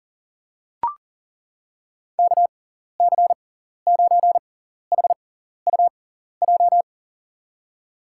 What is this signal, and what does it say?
Morse code, heard as a keyed tone, about 40 wpm, spaced as for 14 wpm, 700 Hz. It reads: KC9HUJ
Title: USA Call Signs - 6 Characters - 6x Character Spacing - 40wpm